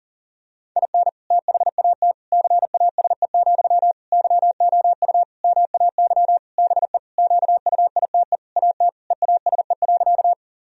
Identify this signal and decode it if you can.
Morse code, heard as a keyed tone, about 40 wpm, 700 Hz. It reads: IN THAT CASE, YOU MAY BE QUITE AT EASE.
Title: Sentences from Top 1000 Words - Review Part 1 - 40wpm